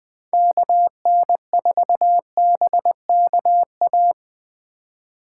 Morse code, heard as a keyed tone, about 24 wpm, 700 Hz. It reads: KN4BKA
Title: Speed Racing - USA 2x3 Call Signs 20wpm